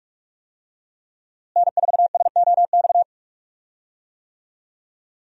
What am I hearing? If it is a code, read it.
Morse code, heard as a keyed tone, about 46 wpm, 700 Hz. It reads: N4SOX
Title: USA Call Signs - 5 Characters - Repeated 3x - 45wpm